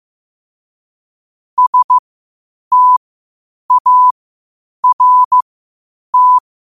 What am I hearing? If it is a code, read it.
Morse code, heard as a keyed tone, about 15 wpm, spaced as for 8 wpm, 1000 Hz. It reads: START
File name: qotdfiles / 8wpm